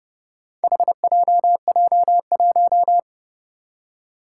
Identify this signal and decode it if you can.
Morse code, heard as a keyed tone, about 29 wpm, 700 Hz. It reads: HJJ1
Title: Four Challenging Letter-Number Combinations 30wpm